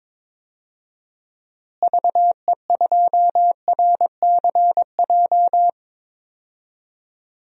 Morse code, heard as a keyed tone, about 22 wpm, 700 Hz. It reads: VE2RCJ
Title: DXCC-100 Call Signs - 6 Characters - Repeated 2x - 22wpm